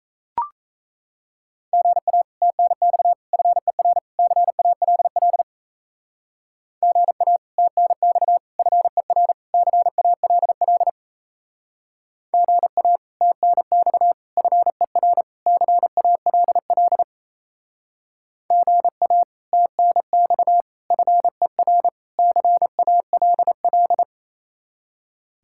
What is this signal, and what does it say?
Morse code, heard as a keyed tone, about 34 wpm, 700 Hz. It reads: GA TNX FER CALL GA TNX FER CALL GA TNX FER CALL GA TNX FER CALL
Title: Speed-Racing Common QSO Phrases - 28wpm